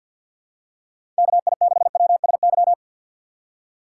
Morse code, heard as a keyed tone, about 53 wpm, 700 Hz. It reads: KI6WSY
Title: USA Call Signs - 6 Characters - 3x Character Spacing - 50wpm